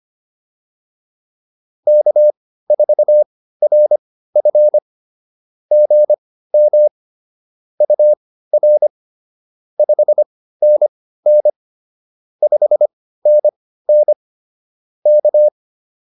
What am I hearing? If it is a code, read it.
Morse code, heard as a keyed tone, about 25 wpm, spaced as for 15 wpm, 600 Hz. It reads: K4RF GM UR 5NN 5NN K